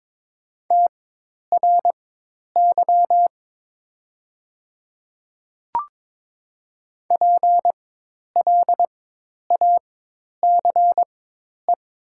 Morse code, heard as a keyed tone, about 21 wpm, spaced as for 10 wpm, 700 Hz. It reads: TRY PLACE
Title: 4x Character Spacing - Top 200 Words - 22wpm